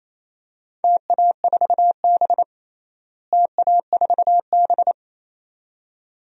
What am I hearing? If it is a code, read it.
Morse code, heard as a keyed tone, about 28 wpm, 700 Hz. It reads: TA4B TA4B
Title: DXCC-100 Call Signs - 4 Characters - Repeated 2x - 28wpm